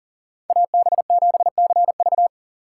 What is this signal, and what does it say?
Morse code, heard as a keyed tone, about 50 wpm, 700 Hz. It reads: AB7CV
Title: Speed Racing - USA Call Signs 40wpm